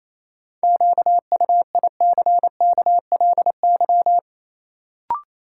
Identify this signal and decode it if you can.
Morse code, heard as a keyed tone, about 28 wpm, 700 Hz. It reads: QUICKLY
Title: Sentences from Top 700 Words - New Word Drill - 28wpm